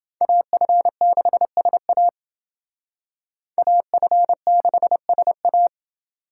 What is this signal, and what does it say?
Morse code, heard as a keyed tone, about 28 wpm, 700 Hz. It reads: AF6SA AF6SA